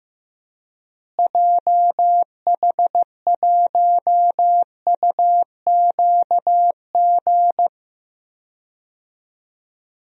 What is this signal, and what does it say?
Morse code, heard as a keyed tone, about 15 wpm, 700 Hz. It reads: JH1UQG
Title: DXCC-100 Call Signs - 6 Characters - Repeated 3x - 15wpm